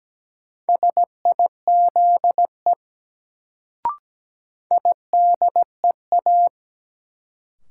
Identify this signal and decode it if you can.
Morse code, heard as a keyed tone, about 17 wpm, 700 Hz. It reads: SIZE IDEA
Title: Sets of 3 Words - New Word Drill - 17wpm